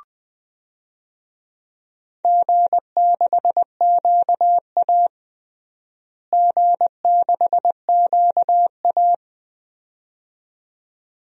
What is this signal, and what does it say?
Morse code, heard as a keyed tone, about 20 wpm, 700 Hz. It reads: G6QA G6QA
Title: DXCC-100 Call Signs - 4 Characters - Repeated 2x - 20wpm